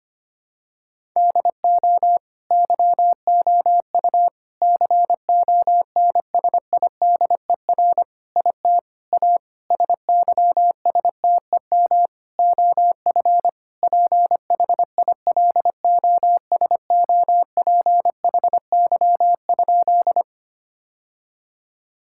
Morse code, heard as a keyed tone, about 25 wpm, 700 Hz. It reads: DO YOU CONSIDER IT A SYSTEM OF PHILOSOPHY?
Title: Sentences from Top 1000 Words - Review Part 6 - 25wpm